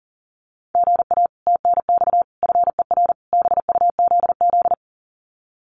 Morse code, heard as a keyed tone, about 40 wpm, 700 Hz. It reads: GA TNX FER BUZZ
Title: Common QSO Phrases Encore 40wpm